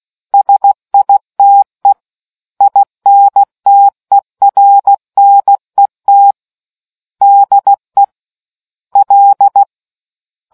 Morse code, heard as a keyed tone, about 16 wpm, 800 Hz. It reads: SITE INTERNET DE L